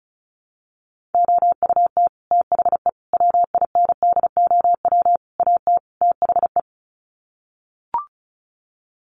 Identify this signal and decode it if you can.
Morse code, heard as a keyed tone, about 34 wpm, 700 Hz. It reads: OUT THE WINDOW AT THE
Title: Sets of 5 Words Encore2 35wpm